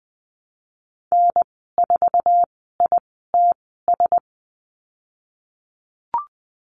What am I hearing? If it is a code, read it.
Morse code, heard as a keyed tone, about 20 wpm, spaced as for 15 wpm, 700 Hz. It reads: N4ITS
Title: 048 Lesson 9g - Number 4 - Call Signs 20wpm